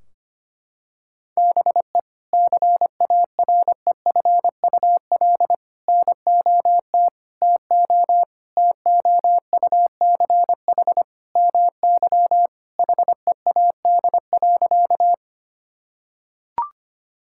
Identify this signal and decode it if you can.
Morse code, heard as a keyed tone, about 25 wpm, 700 Hz. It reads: BE CAREFUL NOT TO TOUCH MY HEAD.
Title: Sentences from Top 1600 Words 25wpm